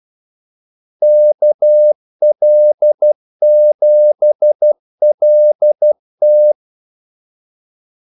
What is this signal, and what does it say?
Morse code, heard as a keyed tone, about 12 wpm, 600 Hz. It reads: KL7LT
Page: 2 ekstra ordmellemrum mellem kaldesignalerne